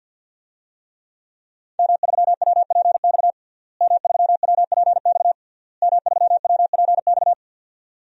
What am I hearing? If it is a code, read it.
Morse code, heard as a keyed tone, about 52 wpm, 700 Hz. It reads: M3WPX M3WPX M3WPX
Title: DXCC-100 Call Signs - 5 Characters - Repeated 3x - 50wpm